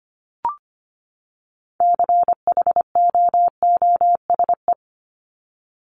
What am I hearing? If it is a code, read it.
Morse code, heard as a keyed tone, about 25 wpm, 700 Hz. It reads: CHOOSE
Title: Top 501-750 Words 25wpm